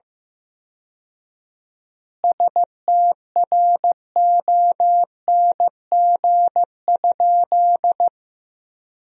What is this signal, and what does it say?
Morse code, heard as a keyed tone, about 15 wpm, 700 Hz. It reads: STRONG?
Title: 4x Word-Spacing - Sentences from Top 500 Words - 15wpm